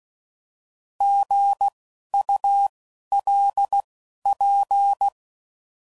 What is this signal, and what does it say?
Morse code, heard as a keyed tone, about 16 wpm, spaced as for 11 wpm, 800 Hz. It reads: GULP
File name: Prefixes / By First Letter